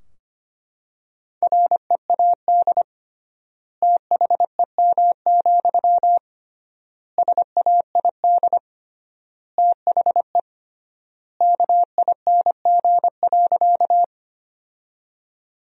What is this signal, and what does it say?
Morse code, heard as a keyed tone, about 25 wpm, 700 Hz. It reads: READ THEM, SAID THE KING.
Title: 3x Word-Spacing - Sentences from Top 400 Words - 25wpm